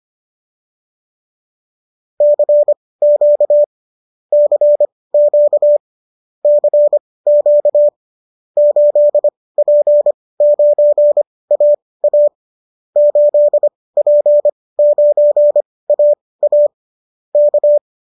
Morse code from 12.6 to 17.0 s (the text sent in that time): 8P9AA